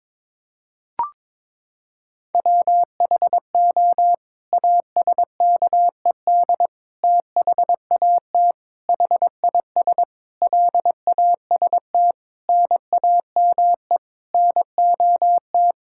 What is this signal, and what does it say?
Morse code, heard as a keyed tone, about 22 wpm, 700 Hz. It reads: WHO ASKED THAT HIS LAST NAME NOT
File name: Sets of 9 Words - 22wpm